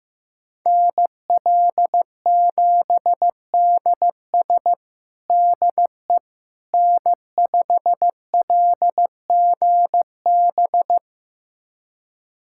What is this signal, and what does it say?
Morse code, heard as a keyed tone, about 16 wpm, 700 Hz. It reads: NL7DS DE N5LGB
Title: USA Callsign DE Callsign - 5 Characters - 15wpm